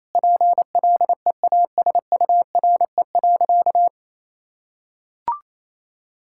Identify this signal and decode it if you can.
Morse code, heard as a keyed tone, about 28 wpm, 700 Hz. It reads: PLEASURE.
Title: Sentences from Top 1600 Words 28wpm